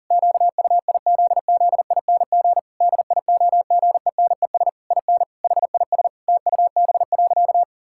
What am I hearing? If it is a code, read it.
Morse code, heard as a keyed tone, about 40 wpm, 700 Hz. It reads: QUIZZING DIOGENES IN HIS TUB.
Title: Pangram Challenge v2 - 40wpm